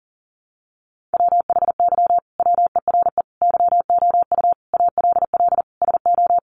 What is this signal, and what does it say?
Morse code, heard as a keyed tone, about 40 wpm, 700 Hz. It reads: WHY WERE YOU ALL SO
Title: Sentences from Top 1400 Words 40wpm